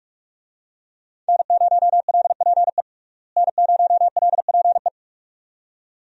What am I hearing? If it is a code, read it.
Morse code, heard as a keyed tone, about 42 wpm, 700 Hz. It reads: N0LPE N0LPE
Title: USA Call Signs - 5 Characters - Repeated 2x - 45wpm